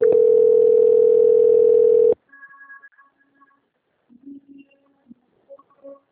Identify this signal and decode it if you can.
Ringback tone (North America), 440 + 480 Hz.
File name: out- - - - - .7751